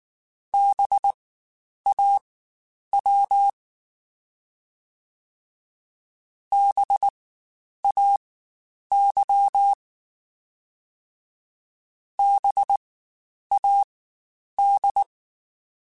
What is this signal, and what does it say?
Morse code, heard as a keyed tone, about 19 wpm, spaced as for 9 wpm, 800 Hz. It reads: BAW BAY BAD
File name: Spaced Alphas / Spaced Alphas mp3s / Bs sp @18